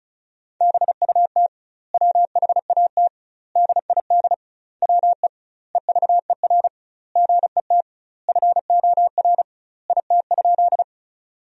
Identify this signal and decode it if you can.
Morse code, heard as a keyed tone, about 34 wpm, 700 Hz. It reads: BUT WHAT DID WE EVER GET FOR IT?
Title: 2x Word-Spacing - Sentences from Top 400 Words - 35wpm